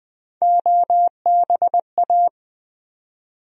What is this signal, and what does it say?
Morse code, heard as a keyed tone, about 20 wpm, 700 Hz. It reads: OBA
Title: USA Call Sign Postfixes - 3 Letters 20wpm